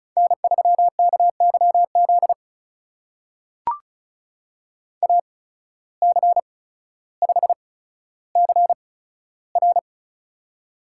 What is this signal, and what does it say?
Morse code, heard as a keyed tone, about 36 wpm, spaced as for 15 wpm, 700 Hz. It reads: N3KYZ AC5CR
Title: USA Call Signs - 5 Characters - 8x Character Spacing - 35wpm